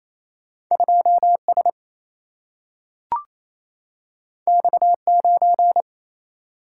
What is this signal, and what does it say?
Morse code, heard as a keyed tone, about 28 wpm, 700 Hz. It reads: 2S X9
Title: Single-Letter Single-Number 28wpm